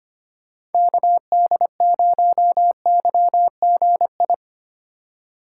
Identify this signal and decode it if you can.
Morse code, heard as a keyed tone, about 31 wpm, 700 Hz. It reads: KD0YGI